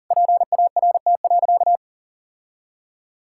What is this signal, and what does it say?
Morse code, heard as a keyed tone, about 37 wpm, 700 Hz. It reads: PART.
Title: Sentences from Top 1000 Words - Review Part 5 - 40wpm